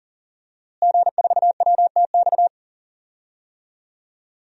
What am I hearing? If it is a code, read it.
Morse code, heard as a keyed tone, about 40 wpm, 700 Hz. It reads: G4WTX
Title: DXCC-100 Call Signs - 5 Characters - Repeated 3x - 40wpm